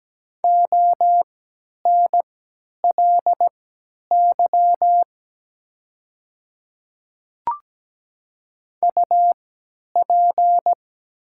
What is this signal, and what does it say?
Morse code, heard as a keyed tone, about 17 wpm, spaced as for 10 wpm, 700 Hz. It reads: ONLY UP
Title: 3x Character Spacing - Top 300 Words - 17wpm